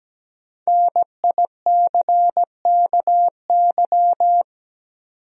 Morse code, heard as a keyed tone, about 17 wpm, 700 Hz. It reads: NICKY
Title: USA Common Names - 17wpm